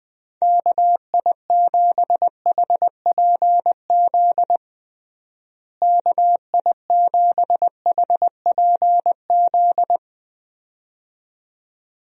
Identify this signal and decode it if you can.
Morse code, heard as a keyed tone, about 21 wpm, 700 Hz. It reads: KI7HPZ KI7HPZ
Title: USA Call Signs - 6 Characters - Repeated 2x - 20wpm